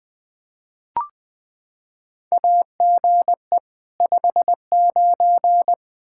Morse code, heard as a keyed tone, about 20 wpm, 700 Hz. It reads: AGE 59
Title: Practice QSO - Exchange 3 - More Info - 20wpm